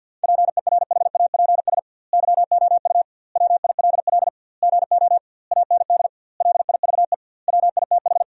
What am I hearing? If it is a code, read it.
Morse code, heard as a keyed tone, about 45 wpm, 700 Hz. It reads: PERHAPS YOU WILL GO AND LIVE WITH
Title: 2x Word-Spacing - Sentences from Top 500 Words - 50wpm